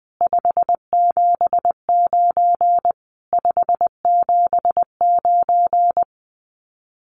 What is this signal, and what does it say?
Morse code, heard as a keyed tone, about 20 wpm, 700 Hz. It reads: 579 579
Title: Common QSO Phrases Encore 20wpm